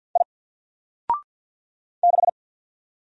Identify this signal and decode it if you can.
Morse code, heard as a keyed tone, about 50 wpm, 700 Hz. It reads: I 6
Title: Single Letter-Number 50wpm